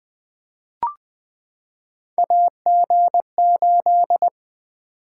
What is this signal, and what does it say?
Morse code, heard as a keyed tone, about 20 wpm, 700 Hz. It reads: AG8
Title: Three Letter-Number Combinations - 20wpm